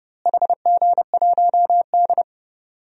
Morse code, heard as a keyed tone, about 30 wpm, 700 Hz. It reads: HG1D
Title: DXCC-100 Call Signs - 4 Characters - Repeated 2x - 30wpm